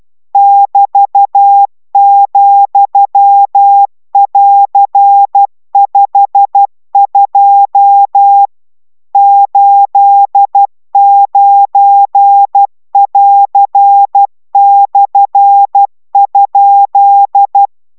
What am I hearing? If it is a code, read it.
Morse code, heard as a keyed tone, about 12 wpm, 800 Hz. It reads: <BT>,<AR>52 89<AR>/?